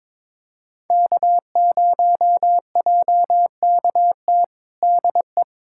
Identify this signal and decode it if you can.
Morse code, heard as a keyed tone, about 22 wpm, 700 Hz. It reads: K0JKT DE
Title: USA Callsign DE Callsign 22wpm